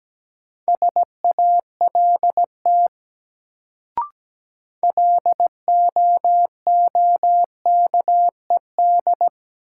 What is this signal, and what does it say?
Morse code, heard as a keyed tone, about 17 wpm, 700 Hz. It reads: SALT LOOKED
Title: Sentences from Top 800 Words - New Word Drill - 17wpm